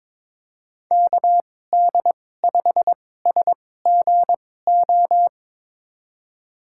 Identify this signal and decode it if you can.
Morse code, heard as a keyed tone, about 22 wpm, spaced as for 16 wpm, 700 Hz. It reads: KD5SGO
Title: USA Call Signs - 6 Characters - 2x Character Spacing - 22wpm